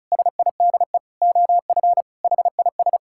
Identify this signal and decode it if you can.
Morse code, heard as a keyed tone, about 34 wpm, 700 Hz. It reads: SIDE OF HIS
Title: Sentences from Top 1600 Words 35wpm